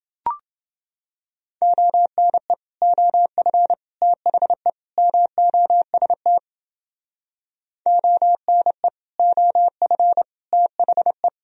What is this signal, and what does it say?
Morse code, heard as a keyed tone, about 29 wpm, 700 Hz. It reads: ONE OF THE MOST ONE OF THE